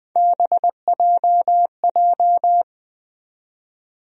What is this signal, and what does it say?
Morse code, heard as a keyed tone, about 21 wpm, 700 Hz. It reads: BJJ
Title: USA Call Sign Postfixes - 3 Letters 20wpm